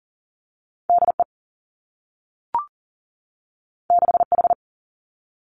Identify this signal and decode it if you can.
Morse code, heard as a keyed tone, about 42 wpm, 700 Hz. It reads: D E 6 H
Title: Two Challenging Letter-Number Combinations 40wpm